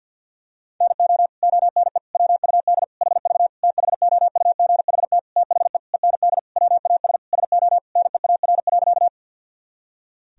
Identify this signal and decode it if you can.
Morse code, heard as a keyed tone, about 46 wpm, 700 Hz. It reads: NO ONE WUD HV THOUGHT THE END WAS SO NEAR.
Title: Sentences from Top 300 Words and Abbreviations 50wpm